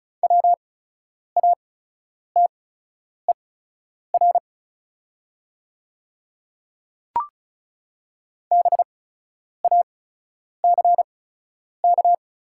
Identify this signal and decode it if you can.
Morse code, heard as a keyed tone, about 36 wpm, spaced as for 10 wpm, 690 Hz. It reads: WATER BACK
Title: 8x Character Spacing - Top 200 Words - 35wpm